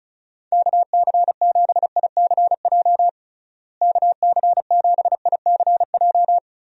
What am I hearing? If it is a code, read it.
Morse code, heard as a keyed tone, about 34 wpm, 700 Hz. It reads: KC7ICJ KC7ICJ
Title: USA Call Signs - 6 Characters - Repeated 3x - 35wpm